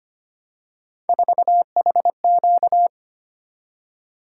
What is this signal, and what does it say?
Morse code, heard as a keyed tone, about 25 wpm, 700 Hz. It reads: 4HQ
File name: Three Letter-Number Combinations - 25wpm